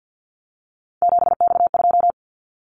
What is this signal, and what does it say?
Morse code, heard as a keyed tone, about 50 wpm, 700 Hz. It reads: 7X2
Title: Callsign Prefixes for Top 100 DXCC - 50wpm